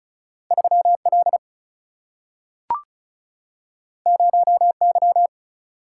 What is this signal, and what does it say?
Morse code, heard as a keyed tone, about 36 wpm, 700 Hz. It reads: 3 L 0 Y
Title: Single-Letter Single-Number 35wpm